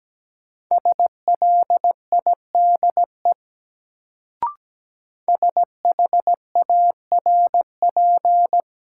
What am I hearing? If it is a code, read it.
Morse code, heard as a keyed tone, about 17 wpm, 700 Hz. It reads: SLIDE SHARP
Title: Sentences from Top 1600 Words - New Word Drill - 17wpm